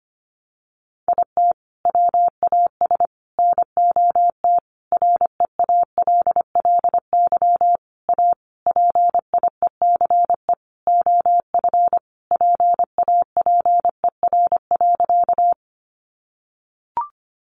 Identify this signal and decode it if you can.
Morse code, heard as a keyed tone, about 25 wpm, 700 Hz. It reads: IT WAS NOT REALLY A PIECE OF PAPER.
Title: Sentences from Top 600 Words 25wpm